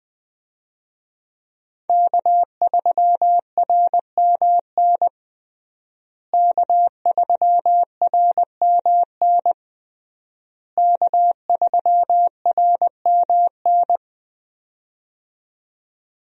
Morse code, heard as a keyed tone, about 21 wpm, 700 Hz. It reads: K3RMN K3RMN K3RMN
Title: USA Call Signs - 5 Characters - Repeated 3x - 20wpm